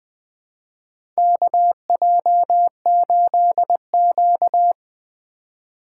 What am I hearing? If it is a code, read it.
Morse code, heard as a keyed tone, about 20 wpm, 700 Hz. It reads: KJ8Q
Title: USA Call Signs - 3x Character Spacing - 20wpm